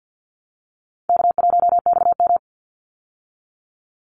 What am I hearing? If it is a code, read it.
Morse code, heard as a keyed tone, about 50 wpm, 700 Hz. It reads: K1XM
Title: Easy Call Signs - 50wpm